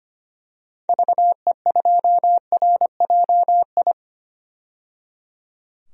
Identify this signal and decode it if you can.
Morse code, heard as a keyed tone, about 25 wpm, 700 Hz. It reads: VE2RJI
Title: DXCC-100 Call Signs - 6 Characters - 25wpm